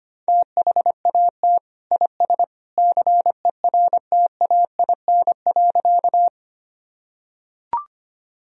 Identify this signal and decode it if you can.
Morse code, heard as a keyed tone, about 25 wpm, 700 Hz. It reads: THAT IS CERTAIN.
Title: Sentences from Top 1000 Words - Review Part 2 - 25wpm